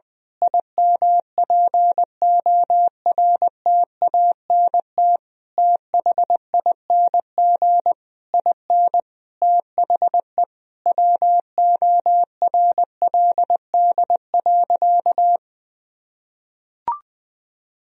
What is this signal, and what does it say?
Morse code, heard as a keyed tone, about 20 wpm, 700 Hz. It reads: IMPORTANT THING IN THE WORLD.
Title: Sentences from Top 1000 Words - Review Part 3 - 20wpm